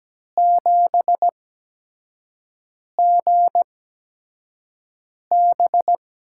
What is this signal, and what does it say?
Morse code, heard as a keyed tone, about 17 wpm, spaced as for 5 wpm, 700 Hz. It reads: 7GB
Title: USA Call Signs - 8x Character Spacing - 17wpm